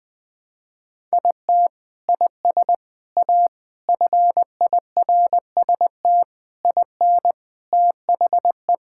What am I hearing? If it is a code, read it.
Morse code, heard as a keyed tone, about 21 wpm, 700 Hz. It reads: IT IS A FIRST IN THE
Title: Sentences from Top 1000 Words - Review Part 6 - 20wpm